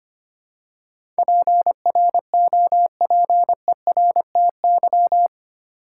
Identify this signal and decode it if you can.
Morse code, heard as a keyed tone, about 25 wpm, 700 Hz. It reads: PROPERTY
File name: Eight Letter Words - 25wpm